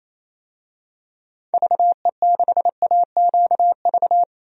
Morse code, heard as a keyed tone, about 28 wpm, 700 Hz. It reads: VE6AQV
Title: DXCC-100 Call Signs - 6 Characters - 28wpm